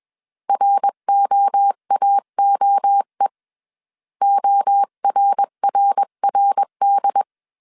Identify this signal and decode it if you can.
Morse code, heard as a keyed tone, about 21 wpm, 800 Hz. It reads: ROAOE ORRRD
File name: 20wpm